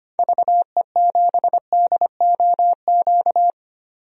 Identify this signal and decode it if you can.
Morse code, heard as a keyed tone, about 25 wpm, 700 Hz. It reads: VE7DOQ
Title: DXCC-100 Call Signs - 6 Characters - 25wpm